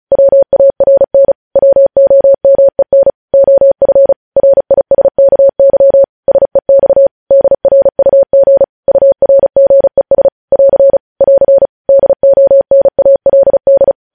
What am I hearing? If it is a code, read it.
Morse code, heard as a keyed tone, about 36 wpm, 550 Hz. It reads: WARN WOMEN OF RISKY SEX DRUG URGES <AR> <AR> DONALD
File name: podcast / News headlines-1x-35x35